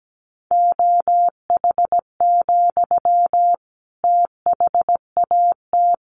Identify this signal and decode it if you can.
Morse code, heard as a keyed tone, about 17 wpm, 700 Hz. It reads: OH, THAT
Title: Sentences from Top 1100 Words 17wpm